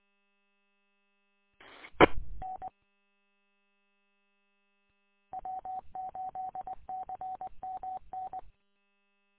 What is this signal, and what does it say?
Morse code, heard as a keyed tone, about 20 wpm, 750 Hz. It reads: E N W8CMN
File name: w8cmn- -combined